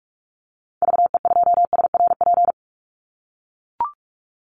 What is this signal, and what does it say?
Morse code, heard as a keyed tone, about 44 wpm, 700 Hz. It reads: VE2SRP
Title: Call Signs - 45wpm